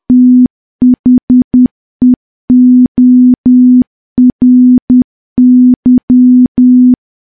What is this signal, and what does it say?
Morse code, heard as a keyed tone, about 10 wpm, 250 Hz. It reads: THEORY
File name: morse / morse-training-10WPM